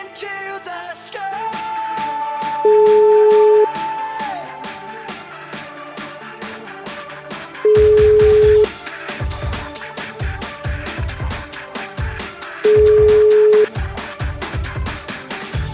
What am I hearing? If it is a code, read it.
Ringback tone (Europe, 425 Hz).